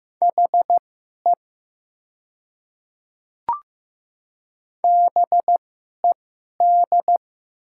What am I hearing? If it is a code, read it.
Morse code, heard as a keyed tone, about 15 wpm, spaced as for 11 wpm, 690 Hz. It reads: HE BED
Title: 2x Character Spacing - Top 500 Words - 15wpm